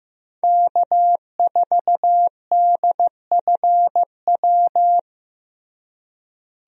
Morse code, heard as a keyed tone, about 16 wpm, 700 Hz. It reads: K4DFW
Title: USA Call Signs - 5 Characters - Repeated 2x - 15wpm